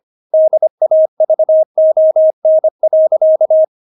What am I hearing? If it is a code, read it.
Morse code, heard as a keyed tone, about 25 wpm, 630 Hz. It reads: DAVON.
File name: Witz6 25wpm